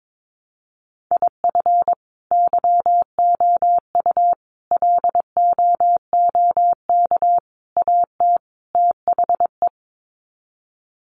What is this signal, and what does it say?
Morse code, heard as a keyed tone, about 22 wpm, 700 Hz. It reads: IF YOU LOOK AT THE
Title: Sets of 5 Words 22wpm